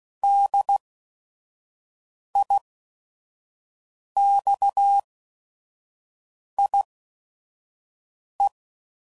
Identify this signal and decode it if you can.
Morse code, heard as a keyed tone, about 16 wpm, spaced as for 5 wpm, 800 Hz. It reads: DIXIE
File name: Prefixes / By First Letter / Ds